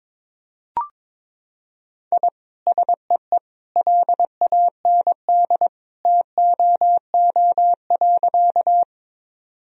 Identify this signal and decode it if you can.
Morse code, heard as a keyed tone, about 22 wpm, 700 Hz. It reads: I SEE LAND TOO.
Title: Sentences from Top 1000 Words - Review Part 1 - 22wpm